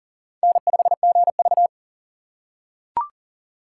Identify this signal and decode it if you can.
Morse code, heard as a keyed tone, about 40 wpm, 700 Hz. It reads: N5GV
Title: USA Call Signs - 4 Characters - 4x Character Spacing - 40wpm